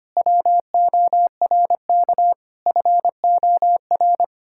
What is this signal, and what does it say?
Morse code, heard as a keyed tone, about 25 wpm, 700 Hz. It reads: WORK FOR
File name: practice / 25wpm / Difficult Sentences with Hints - 25wpm